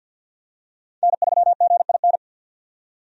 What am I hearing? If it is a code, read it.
Morse code, heard as a keyed tone, about 51 wpm, 700 Hz. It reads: N3GIN
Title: USA Call Signs - 5 Characters - 50wpm